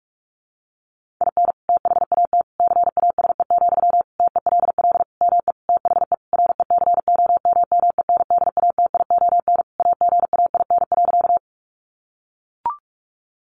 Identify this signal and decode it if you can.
Morse code, heard as a keyed tone, about 44 wpm, 700 Hz. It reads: IN THAT CASE, TELL ME THE RECOMMENDATION AGAIN.
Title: Sentences from Top 1400 Words 45wpm